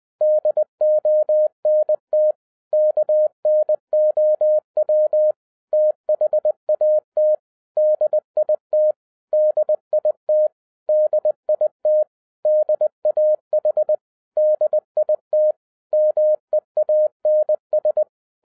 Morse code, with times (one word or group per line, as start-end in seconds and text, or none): 0.2-2.3 DONT
2.7-5.3 KNOW
5.7-7.3 THAT
7.8-8.9 DIT
9.3-10.5 DIT
10.9-12.0 DIT
12.5-13.9 DAH
14.4-15.5 DIT
15.9-18.0 MEANS